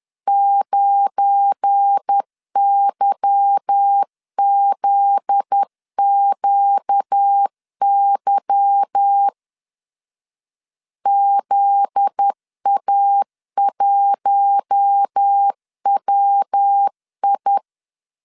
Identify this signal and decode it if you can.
Morse code, heard as a keyed tone, about 10 wpm, 800 Hz. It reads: 9YZQY ZA1WI
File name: LEZIONI / 8 YJ45 / 8 Y 10wpm 2615